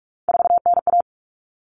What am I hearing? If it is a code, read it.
Morse code, heard as a keyed tone, about 44 wpm, 700 Hz. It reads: 4NA
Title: Common Cut Numbers - 3-Digit 45wpm